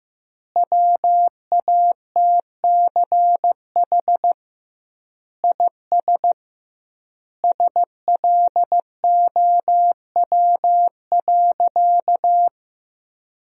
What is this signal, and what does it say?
Morse code, heard as a keyed tone, about 16 wpm, 700 Hz. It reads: WATCH IS SLOW.
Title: 2x Word-Spacing - Sentences from Top 400 Words - 15wpm